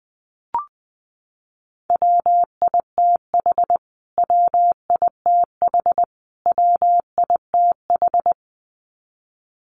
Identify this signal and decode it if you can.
Morse code, heard as a keyed tone, about 20 wpm, 700 Hz. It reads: WITH WITH WITH
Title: Top 200 Words Repeated 20wpm